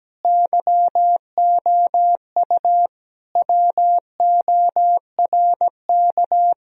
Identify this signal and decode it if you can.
Morse code, heard as a keyed tone, about 17 wpm, 700 Hz. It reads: YOU WORK
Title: Sentences from Top 1000 Words - Review Part 4 - 17wpm